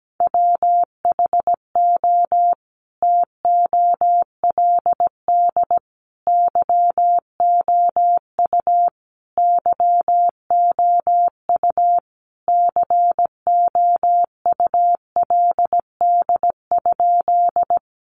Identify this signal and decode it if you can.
Morse code, heard as a keyed tone, about 17 wpm, 700 Hz. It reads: WHO TOLD YOU YOU COULD?
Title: Sentences from Top 400 Words 17wpm